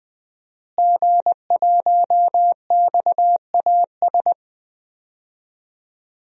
Morse code, heard as a keyed tone, about 20 wpm, 700 Hz. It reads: G1XAS
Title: DXCC-100 Call Signs - 5 Characters - Repeated 3x - 20wpm